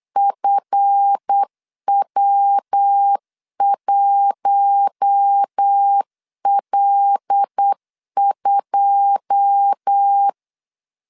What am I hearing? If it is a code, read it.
Morse code, heard as a keyed tone, about 8 wpm, 800 Hz. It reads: FW1L2
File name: LEZIONI / 10 SIGNS / 10D COMA 8wpm 3749